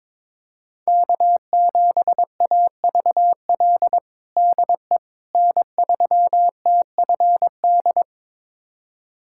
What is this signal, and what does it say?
Morse code, heard as a keyed tone, about 21 wpm, 700 Hz. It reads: K7AVL DE N3TFD
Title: USA Callsign DE Callsign - 5 Characters - 22wpm